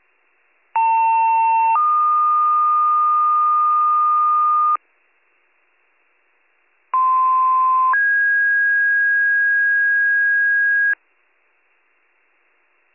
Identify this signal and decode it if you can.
Two-tone page, heard as a one-second tone, then a three-second tone, about 900 then 1220 Hz, about 1040 then 1650 Hz. It is heard from 0.8 to 10.9 s.